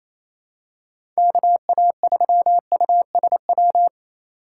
Morse code, heard as a keyed tone, about 28 wpm, 700 Hz. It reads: KA3USW
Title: USA Call Signs - 6 Characters - 4x Character Spacing - 28wpm